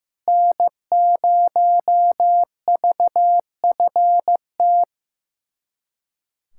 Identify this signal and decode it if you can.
Morse code, heard as a keyed tone, about 16 wpm, 700 Hz. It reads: N0VFT
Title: USA Callsign DE Callsign 15wpm